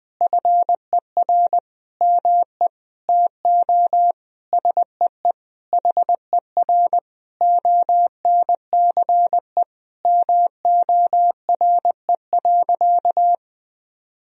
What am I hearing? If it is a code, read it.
Morse code, heard as a keyed tone, about 20 wpm, 700 Hz. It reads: FER ME TO SEE HER ONCE MORE.
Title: Sentences from Top 500 Words and Abbreviations 20wpm